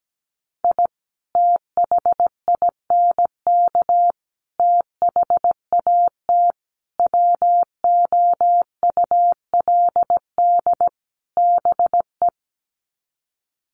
Morse code, heard as a keyed tone, about 17 wpm, 700 Hz. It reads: I THINK THAT WOULD BE
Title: Sets of 5 Words Encore2 17wpm